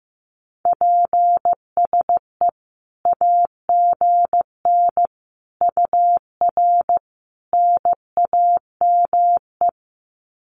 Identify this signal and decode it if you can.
Morse code, heard as a keyed tone, about 14 wpm, 700 Hz. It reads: PSE AGN UR NAME
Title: Common QSO Phrases 15wpm